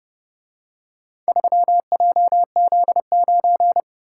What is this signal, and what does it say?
Morse code, heard as a keyed tone, about 28 wpm, 700 Hz. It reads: 3JZ9